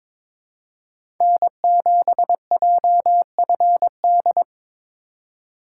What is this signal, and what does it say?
Morse code, heard as a keyed tone, about 22 wpm, 700 Hz. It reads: N7JFD
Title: USA Call Signs - 5 Characters - 2x Character Spacing - 22wpm